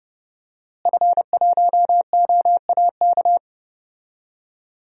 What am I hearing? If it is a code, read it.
Morse code, heard as a keyed tone, about 30 wpm, 700 Hz. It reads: F1OAK
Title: DXCC-100 Call Signs - 5 Characters - Repeated 3x - 30wpm